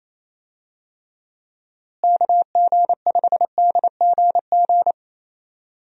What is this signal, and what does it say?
Morse code, heard as a keyed tone, about 28 wpm, 700 Hz. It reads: KG5DGG
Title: USA Call Signs - 6 Characters - Repeated 3x - 28wpm